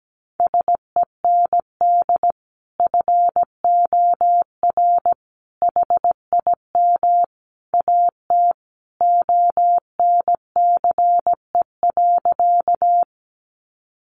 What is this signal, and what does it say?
Morse code, heard as a keyed tone, about 17 wpm, 700 Hz. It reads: SEND FOR HIM AT ONCE.
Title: Sentences from Top 1500 Words 17wpm